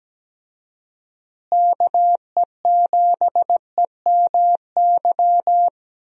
Morse code, heard as a keyed tone, about 17 wpm, 700 Hz. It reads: KE7EMY
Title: USA Call Signs - 6 Characters - Repeated 3x - 17wpm